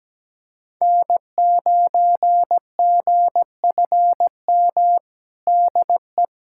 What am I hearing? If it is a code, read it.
Morse code, heard as a keyed tone, about 17 wpm, 700 Hz. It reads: N9GFM DE
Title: USA Callsign DE Callsign 17wpm